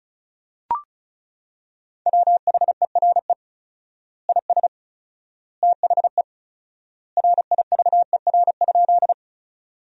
Morse code, heard as a keyed tone, about 36 wpm, 700 Hz. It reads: WHERE IS THE RIVER?
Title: 4x Word-Spacing - Sentences from Top 300 Words - 35wpm